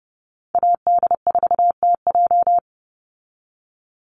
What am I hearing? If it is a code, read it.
Morse code, heard as a keyed tone, about 30 wpm, 700 Hz. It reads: AD4TJ
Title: Call Signs Repeated 3x - 30wpm